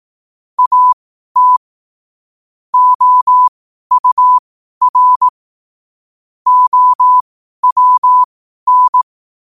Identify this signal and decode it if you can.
Morse code, heard as a keyed tone, about 18 wpm, spaced as for 12 wpm, 1000 Hz. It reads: AT OUR OWN
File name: qotdfiles / 12wpm